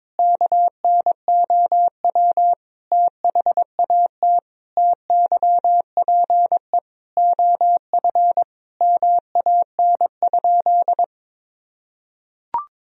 Morse code, heard as a keyed tone, about 22 wpm, 700 Hz. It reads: KNOW THAT TYPE OF MAN?
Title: Sentences from Top 1000 Words - Review Part 3 - 22wpm